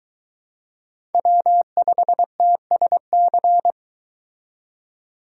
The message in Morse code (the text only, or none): W5TSC